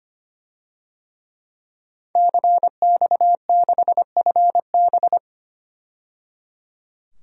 Morse code, heard as a keyed tone, about 25 wpm, 700 Hz. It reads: CX6FB
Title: DXCC-100 Call Signs - 5 Characters - 25wpm